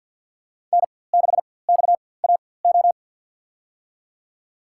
Morse code, heard as a keyed tone, about 47 wpm, spaced as for 23 wpm, 700 Hz. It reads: N6XAO
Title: USA Call Signs - 5 Characters - 4x Character Spacing - 50wpm